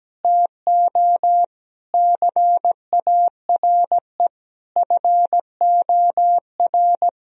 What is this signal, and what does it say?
Morse code, heard as a keyed tone, about 17 wpm, 700 Hz. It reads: TO CARE FOR
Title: Sentences from Top 1000 Words - Review Part 3 - 17wpm